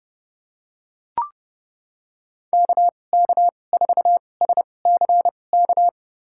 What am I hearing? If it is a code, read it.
Morse code, heard as a keyed tone, about 30 wpm, spaced as for 22 wpm, 700 Hz. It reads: KK4SCK
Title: USA Call Signs - 6 Characters - 2x Character Spacing - 30wpm